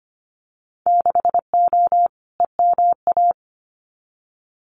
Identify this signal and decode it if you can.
Morse code, heard as a keyed tone, about 25 wpm, 700 Hz. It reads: 6O EMA
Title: Winter Field Day Exchanges Encore 25wpm